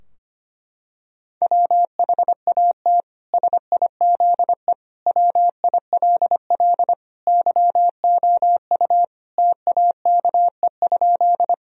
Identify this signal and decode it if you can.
Morse code, heard as a keyed tone, about 25 wpm, 700 Hz. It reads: WHAT SIZE WILL YOU TAKE?
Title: Sentences from Top 1000 Words - Review Part 2 - 25wpm